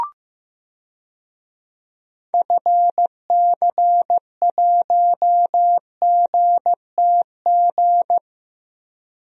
Morse code, heard as a keyed tone, about 15 wpm, 700 Hz. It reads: FC1GTG
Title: DXCC-100 Call Signs - 6 Characters - Repeated 3x - 15wpm